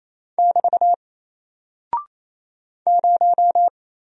Morse code, heard as a keyed tone, about 27 wpm, 700 Hz. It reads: <BT> 0
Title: Single Character (Letter:Number:Punctuation:Prosign) 28wpm